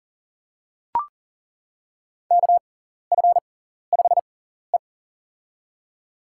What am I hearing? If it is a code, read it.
Morse code, heard as a keyed tone, about 37 wpm, spaced as for 14 wpm, 700 Hz. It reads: KF5E
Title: USA Call Signs - 4 Characters - 6x Character Spacing - 40wpm